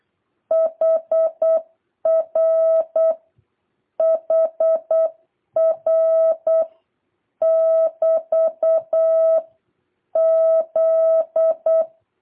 Morse code, heard as a keyed tone, about 8 wpm, 650 Hz. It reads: HR HR <BT> Z